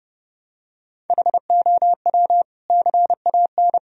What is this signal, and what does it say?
Morse code, heard as a keyed tone, about 29 wpm, 700 Hz. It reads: HOW CAN
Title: Sentences from Top 1000 Words - Review Part 2 - 30wpm